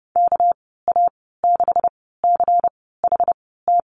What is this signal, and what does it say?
Morse code, heard as a keyed tone, about 30 wpm, spaced as for 17 wpm, 700 Hz. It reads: KA6CHT
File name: Lesson 37g - Letter x - Call Signs 30wpm